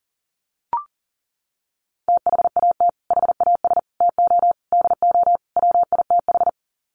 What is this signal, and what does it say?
Morse code, heard as a keyed tone, about 40 wpm, 700 Hz. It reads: THAT HAS TO DO WITH
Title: Sets of 5 Words Encore2 40wpm